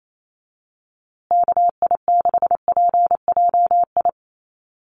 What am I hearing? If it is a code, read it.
Morse code, heard as a keyed tone, about 28 wpm, 690 Hz. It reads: KI6PJI
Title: USA Call Signs Repeated 3x 28wpm